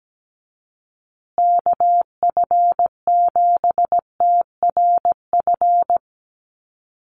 Morse code, heard as a keyed tone, about 17 wpm, 700 Hz. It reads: KF7TRF
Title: USA Call Signs Repeated 2x 17wpm